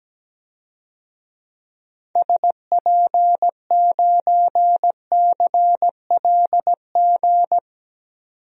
Morse code, heard as a keyed tone, about 17 wpm, 700 Hz. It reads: SP9CLG
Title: DXCC-100 Call Signs - 6 Characters - 17wpm